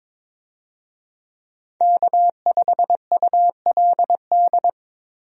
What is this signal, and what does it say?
Morse code, heard as a keyed tone, about 22 wpm, 700 Hz. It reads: K5ULD
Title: USA Call Signs - 5 Characters - Repeated 2x - 22wpm